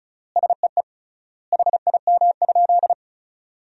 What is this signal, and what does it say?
Morse code, heard as a keyed tone, about 34 wpm, 700 Hz. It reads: SEE HIM?
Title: 3x Word-Spacing - Sentences from Top 100 Words - 35wpm